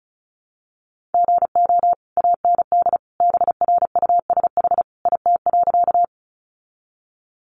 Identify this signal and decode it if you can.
Morse code, heard as a keyed tone, about 34 wpm, 700 Hz. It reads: GO AND BRUSH IT.
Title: Sentences from Top 1100 Words 35wpm